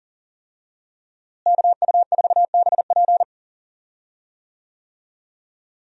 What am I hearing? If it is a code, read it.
Morse code, heard as a keyed tone, about 40 wpm, 700 Hz. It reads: KU4BP
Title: USA Call Signs - 5 Characters - Repeated 2x - 40wpm